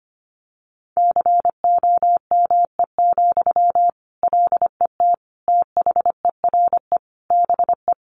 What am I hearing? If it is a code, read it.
Morse code, heard as a keyed tone, about 25 wpm, 700 Hz. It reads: COME, LET THERE BE
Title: Sentences from Top 1500 Words 25wpm